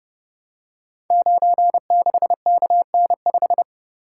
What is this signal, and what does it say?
Morse code, heard as a keyed tone, about 30 wpm, 700 Hz. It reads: 96KN5
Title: Code Groups - 5 Letter:Numbers - 30wpm